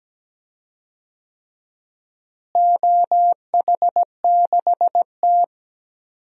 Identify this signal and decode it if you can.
Morse code, heard as a keyed tone, about 17 wpm, 700 Hz. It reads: OH6T
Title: DXCC-100 Call Signs - 4 Characters - Repeated 3x - 17wpm